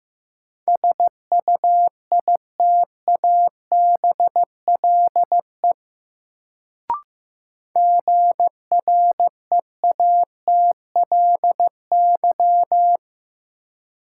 Morse code, heard as a keyed tone, about 15 wpm, 700 Hz. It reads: SUITABLE GREATLY
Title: Sentences from Top 1400 Words - New Word Drill Alt - 15wpm